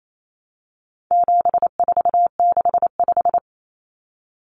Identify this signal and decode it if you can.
Morse code, heard as a keyed tone, about 28 wpm, 700 Hz. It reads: 7465
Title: Numbers - 4-Digit 28wpm